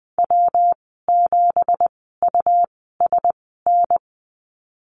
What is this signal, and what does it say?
Morse code, heard as a keyed tone, about 20 wpm, spaced as for 15 wpm, 700 Hz. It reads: W7USN
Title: 179 Lesson 29g - Slash - Call Signs 20wpm